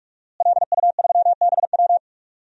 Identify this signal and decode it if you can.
Morse code, heard as a keyed tone, about 46 wpm, 700 Hz. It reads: LU3BW
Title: DXCC-100 Call Signs - 5 Characters - Repeated 3x - 45wpm